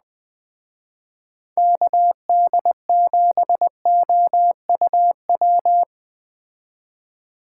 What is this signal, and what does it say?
Morse code, heard as a keyed tone, about 20 wpm, 700 Hz. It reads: KD7OUW